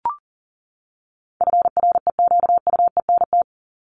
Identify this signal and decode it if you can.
Morse code, heard as a keyed tone, about 38 wpm, 700 Hz. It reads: FREQUENT
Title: Top 1251-1500 Words 40wpm